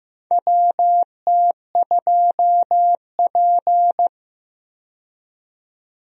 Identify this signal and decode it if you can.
Morse code, heard as a keyed tone, about 15 wpm, 700 Hz. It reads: WT2P
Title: USA Call Signs - 4 Characters - 15wpm